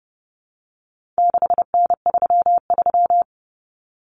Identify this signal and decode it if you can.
Morse code, heard as a keyed tone, about 30 wpm, 700 Hz. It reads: 6N33
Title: Common Cut Numbers - 4-Digit 30wpm